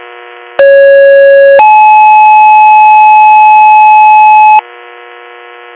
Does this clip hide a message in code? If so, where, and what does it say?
Yes: two-tone page from 0.6 to 4.6 s, heard as a one-second tone, then a three-second tone, about 550 then 860 Hz.